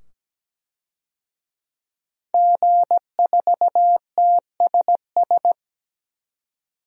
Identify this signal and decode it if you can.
Morse code, heard as a keyed tone, about 17 wpm, 700 Hz. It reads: G4TSS
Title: DXCC-100 Call Signs - 5 Characters - 17wpm